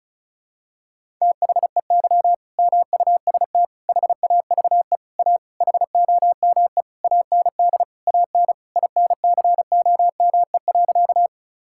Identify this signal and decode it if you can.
Morse code, heard as a keyed tone, about 36 wpm, 700 Hz. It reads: THEY MUST HAVE A HOME AND AN INCOME.
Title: Sentences from Top 1000 Words - Review Part 5 - 35wpm